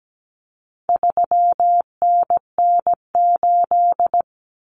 Morse code, heard as a keyed tone, about 17 wpm, 700 Hz. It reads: 3NN8
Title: Common Cut Numbers - 4-Digit 17wpm